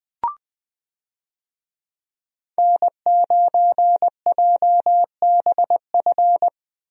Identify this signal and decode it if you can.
Morse code, heard as a keyed tone, about 20 wpm, 700 Hz. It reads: N9JBF
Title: USA Call Signs - 5 Characters - Repeated 2x - 20wpm